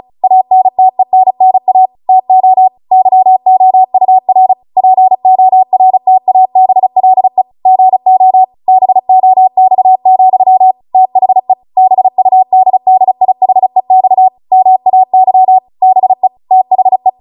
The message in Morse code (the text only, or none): ANTENNA TO YOUR PORTABLE GO BOX, THE BUDDIHEX MAY BE THE